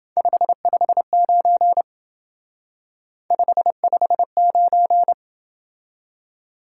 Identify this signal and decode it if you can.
Morse code, heard as a keyed tone, about 28 wpm, 700 Hz. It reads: 559 559